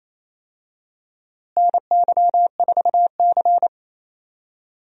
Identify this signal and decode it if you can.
Morse code, heard as a keyed tone, about 28 wpm, 700 Hz. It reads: NY4C
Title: USA Call Signs - 4 Characters - Repeated 3x - 28wpm